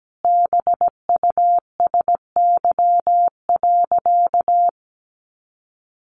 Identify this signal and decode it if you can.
Morse code, heard as a keyed tone, about 17 wpm, 700 Hz. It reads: BUSY.
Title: Additional Easy Phrases 17wpm